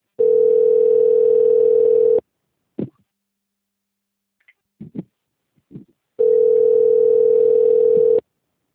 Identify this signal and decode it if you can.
Ringback tone (North America), 440 + 480 Hz.
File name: out- - - - - .23745